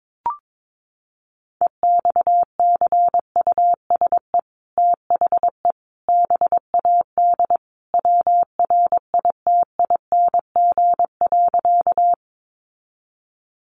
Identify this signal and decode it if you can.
Morse code, heard as a keyed tone, about 22 wpm, 700 Hz. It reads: EXCUSE THE BAD WRITING.
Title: Sentences from Top 800 Words 22wpm